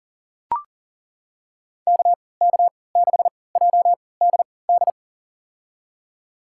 Morse code, heard as a keyed tone, about 40 wpm, spaced as for 23 wpm, 700 Hz. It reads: KK6JDD
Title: USA Call Signs - 3x Character Spacing - 40wpm